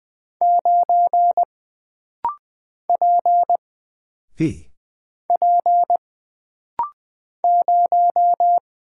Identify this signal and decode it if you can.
Morse code, heard as a keyed tone, about 20 wpm, 700 Hz. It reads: 9 P P 0
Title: Single Character (Letter:Number:Punctuation:Prosign) - Rapid-Fire - 20wpm